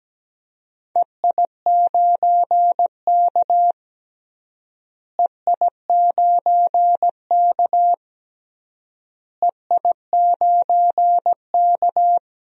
Morse code, heard as a keyed tone, about 17 wpm, 700 Hz. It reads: EI9K EI9K EI9K
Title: DXCC-100 Call Signs - 4 Characters - Repeated 3x - 17wpm